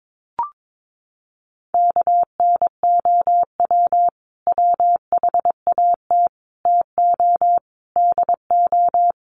Morse code, heard as a keyed tone, about 22 wpm, 700 Hz. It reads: KNOW WHAT TO DO
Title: Sets of 4 Words 22wpm